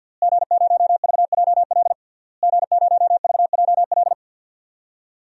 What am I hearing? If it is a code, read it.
Morse code, heard as a keyed tone, about 52 wpm, 700 Hz. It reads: G0VJL G0VJL
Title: DXCC-100 Call Signs - 5 Characters - Repeated 3x - 50wpm